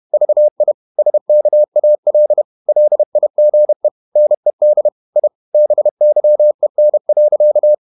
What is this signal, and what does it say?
Morse code, heard as a keyed tone, about 31 wpm, 600 Hz. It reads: VI SKAL LIGE NED I BYEN.
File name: Ferie QRQ 31wpm